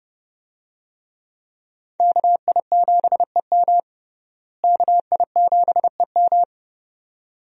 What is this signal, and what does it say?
Morse code, heard as a keyed tone, about 32 wpm, 700 Hz. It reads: KI7EM KI7EM
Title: USA Call Signs - 5 Characters - Repeated 2x - 30wpm